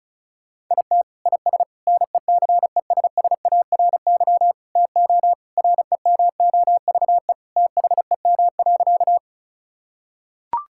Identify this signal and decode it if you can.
Morse code, heard as a keyed tone, about 36 wpm, 700 Hz. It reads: IT IS NECESSARY TO REMOVE THEM.
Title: Sentences from Top 1000 Words - Review Part 5 - 35wpm